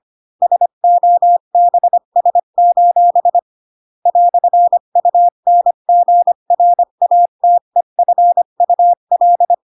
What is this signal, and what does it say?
Morse code, heard as a keyed tone, about 25 wpm, 700 Hz. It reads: SOBS: "UNGRATEFUL
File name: Pinocchio / Pinocchio ch3 20 25 ik1hge